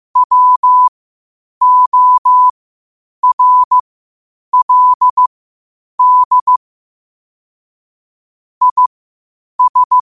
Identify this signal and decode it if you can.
Morse code, heard as a keyed tone, about 15 wpm, spaced as for 8 wpm, 1000 Hz. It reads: WORLD IS
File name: qotdfiles / 8wpm